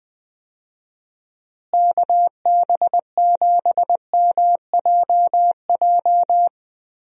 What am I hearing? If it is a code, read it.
Morse code, heard as a keyed tone, about 20 wpm, 700 Hz. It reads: KB7MJJ
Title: USA Call Signs - 6 Characters - Repeated 3x - 20wpm